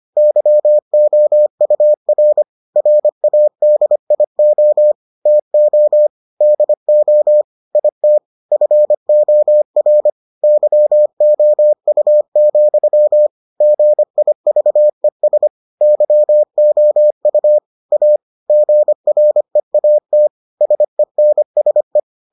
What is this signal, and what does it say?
Morse code, heard as a keyed tone, about 25 wpm, 600 Hz. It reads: YOUR RADIO TO DO IT FOR YOU, GIVES YOU A GREAT SENSE